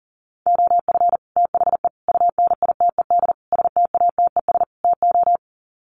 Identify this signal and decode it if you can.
Morse code, heard as a keyed tone, about 40 wpm, 700 Hz. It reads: OF THE UNITED STATES TO
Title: Sets of 5 Words Encore 40wpm